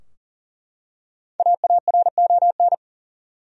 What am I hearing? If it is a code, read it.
Morse code, heard as a keyed tone, about 40 wpm, 700 Hz. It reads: AARON
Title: USA Common Names - Max 5 Letters - 40wpm